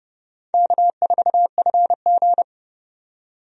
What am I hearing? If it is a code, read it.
Morse code, heard as a keyed tone, about 32 wpm, 700 Hz. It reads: K4FG
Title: USA Call Signs - 4 Characters - 30wpm